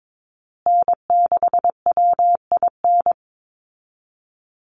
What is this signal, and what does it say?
Morse code, heard as a keyed tone, about 22 wpm, 700 Hz. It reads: N6WIN
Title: Call Signs Repeated 2x - 22wpm